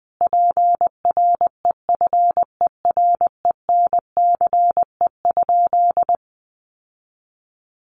Morse code, heard as a keyed tone, about 20 wpm, 700 Hz. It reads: PREFERENCE?
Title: Sentences from Top 1500 Words 20wpm